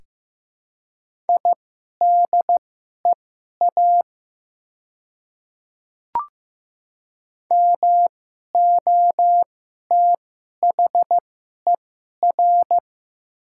Morse code, heard as a keyed tone, about 15 wpm, spaced as for 11 wpm, 700 Hz. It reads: IDEA MOTHER
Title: 2x Character Spacing - Top 300 Words - 15wpm